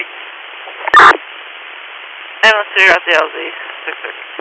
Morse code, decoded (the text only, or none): none